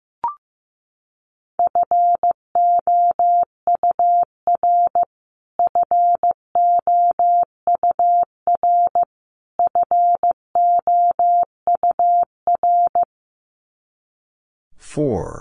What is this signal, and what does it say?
Morse code, heard as a keyed tone, about 15 wpm, 700 Hz. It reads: FOUR FOUR FOUR
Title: Top 300 Words Repeated 15wpm